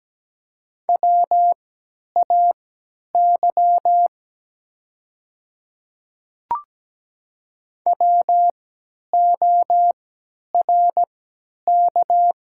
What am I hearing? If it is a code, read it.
Morse code, heard as a keyed tone, about 17 wpm, spaced as for 10 wpm, 700 Hz. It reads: WAY WORK
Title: 3x Character Spacing - Top 200 Words - 17wpm